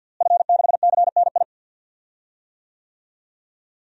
Morse code, heard as a keyed tone, about 50 wpm, 700 Hz. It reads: F6CNI
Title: DXCC-100 Call Signs - 5 Characters - Repeated 2x - 50wpm